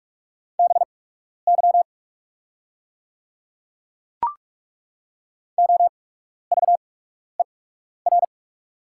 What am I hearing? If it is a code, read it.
Morse code, heard as a keyed tone, about 43 wpm, spaced as for 12 wpm, 690 Hz. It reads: BY OVER
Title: 8x Character Spacing - Top 500 Words - 45wpm